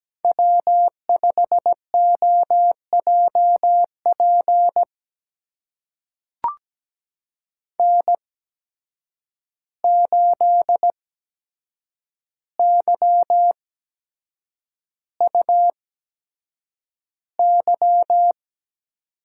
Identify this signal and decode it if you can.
Morse code, heard as a keyed tone, about 17 wpm, spaced as for 7 wpm, 700 Hz. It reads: W5OJP N8YUY
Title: USA Call Signs - 8x Character Spacing - 17wpm